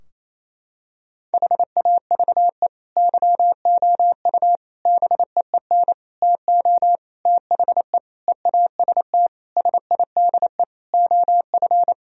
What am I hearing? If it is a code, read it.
Morse code, heard as a keyed tone, about 28 wpm, 700 Hz. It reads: HAVE YOU BEEN TO THE EAST SIDE OF
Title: Sentences from Top 1000 Words - Review Part 3 - 28wpm